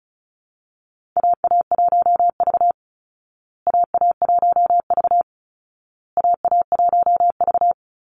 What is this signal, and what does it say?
Morse code, heard as a keyed tone, about 34 wpm, 700 Hz. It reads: AA1V AA1V AA1V
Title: Call Signs Repeated 2x - 35wpm